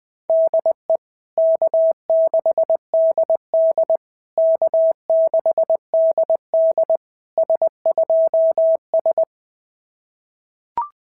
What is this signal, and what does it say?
Morse code, heard as a keyed tone, about 20 wpm, 650 Hz. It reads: DE K6DD K6DD S2S
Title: SOTA QSO Practice 20wpm